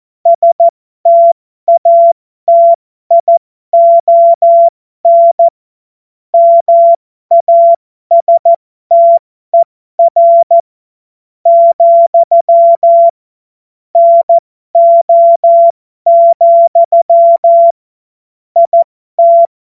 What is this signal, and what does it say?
Morse code, heard as a keyed tone, about 14 wpm, spaced as for 12 wpm, 680 Hz. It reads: STATION MASTER, NO, IT
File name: Morsekurs / Witze